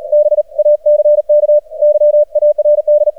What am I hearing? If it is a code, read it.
Morse code, heard as a keyed tone, about 40 wpm, 600 Hz. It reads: BACK YARD
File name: archived / CRSnet